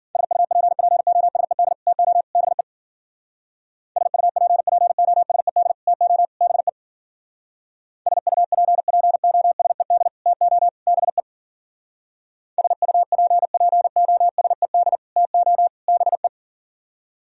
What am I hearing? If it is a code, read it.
Morse code, heard as a keyed tone, about 52 wpm, 700 Hz. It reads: SUPPOSED TO BE SUPPOSED TO BE SUPPOSED TO BE SUPPOSED TO BE